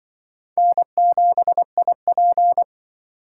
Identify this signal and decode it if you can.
Morse code, heard as a keyed tone, about 25 wpm, 700 Hz. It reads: N7IP
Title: Speed Racing - USA 1x2 Call Signs 20wpm